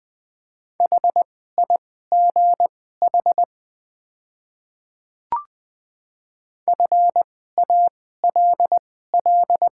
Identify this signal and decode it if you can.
Morse code, heard as a keyed tone, about 20 wpm, spaced as for 14 wpm, 700 Hz. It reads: HIGH FALL
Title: 2x Character Spacing - Top 500 Words - 20wpm